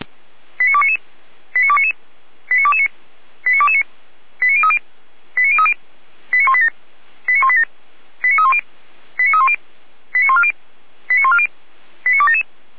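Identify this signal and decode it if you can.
Five-tone selective call (ZVEI), heard as quick runs of five tones, tones of about 80 ms. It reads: 89299 89299 89229 89229 89030 89030 89187 89187 89310 89310 89140 89140 89277
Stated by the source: ZVEI five-tone call 89174 received by SDR.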